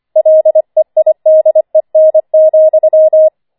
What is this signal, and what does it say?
Morse code, heard as a keyed tone, about 25 wpm, 630 Hz. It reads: LEIDEN,
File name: 25wpm